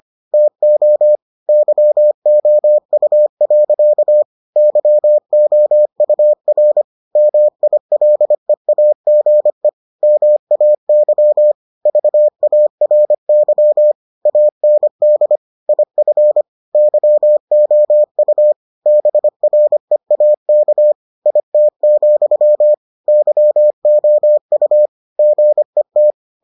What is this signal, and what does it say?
Morse code, heard as a keyed tone, about 25 wpm, 600 Hz. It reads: TO YOU. YOUR MILEAGE MAY VARY AND IF YOU BREAK IT, YOU GET